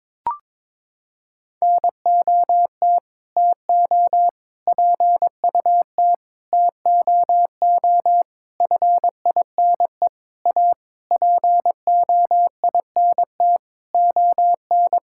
Morse code, heard as a keyed tone, about 22 wpm, 700 Hz. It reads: NOT TO PUT TOO FINE A POINT ON
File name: practice / 22wpm / Sentences with 8-Word Phrases - 22wpm